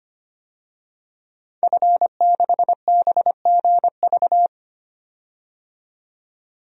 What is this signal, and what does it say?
Morse code, heard as a keyed tone, about 25 wpm, 700 Hz. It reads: F6BGV
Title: DXCC-100 Call Signs - 5 Characters - 25wpm